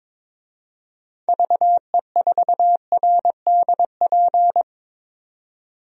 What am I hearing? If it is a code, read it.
Morse code, heard as a keyed tone, about 22 wpm, 700 Hz. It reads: VE4RDP
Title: DXCC-100 Call Signs - 6 Characters - 22wpm